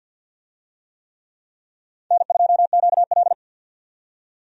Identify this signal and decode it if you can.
Morse code, heard as a keyed tone, about 46 wpm, 700 Hz. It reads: N2QL
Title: USA Call Signs - 4 Characters - 50wpm